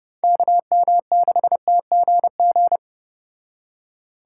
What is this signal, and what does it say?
Morse code, heard as a keyed tone, about 31 wpm, 700 Hz. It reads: KM6TGG